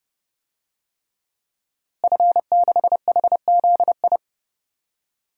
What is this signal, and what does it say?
Morse code, heard as a keyed tone, about 31 wpm, 700 Hz. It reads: F6HZI
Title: DXCC-100 Call Signs - 5 Characters - Repeated 3x - 30wpm